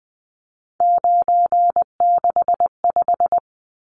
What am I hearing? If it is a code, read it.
Morse code, heard as a keyed tone, about 20 wpm, 700 Hz. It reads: 965
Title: Numbers - 3-Digit 20wpm